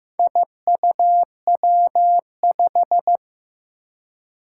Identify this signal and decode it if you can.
Morse code, heard as a keyed tone, about 15 wpm, 700 Hz. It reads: IUW5
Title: Four Challenging Letter-Number Combinations 15wpm